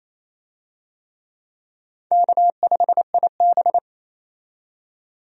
Morse code, heard as a keyed tone, about 28 wpm, 700 Hz. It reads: K5IB
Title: USA Call Signs - 4 Characters - 28wpm